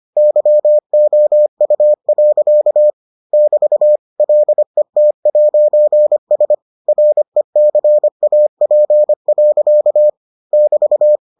Morse code, heard as a keyed tone, about 25 wpm, 600 Hz. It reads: YOU. <BT> LET'S RECAP. <BT>